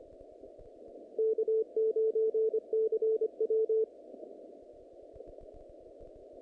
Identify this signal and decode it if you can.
Morse code, heard as a keyed tone, about 25 wpm, 440 Hz. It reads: K9CW
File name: contests / naqp-cw-k5tr-op-k5pi- -2200